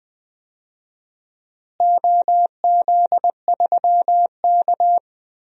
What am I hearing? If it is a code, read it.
Morse code, heard as a keyed tone, about 20 wpm, 700 Hz. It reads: OZ3K
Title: DXCC-100 Call Signs - 4 Characters - Repeated 2x - 20wpm